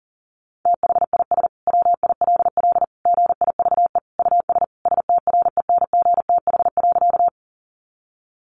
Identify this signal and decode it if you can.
Morse code, heard as a keyed tone, about 40 wpm, 700 Hz. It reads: THIS WILL GIVE US STRENGTH.
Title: Sentences from Top 1300 Words 40wpm